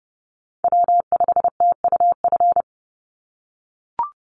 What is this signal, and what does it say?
Morse code, heard as a keyed tone, about 30 wpm, 700 Hz. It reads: W5TUF
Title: Call Signs - 30wpm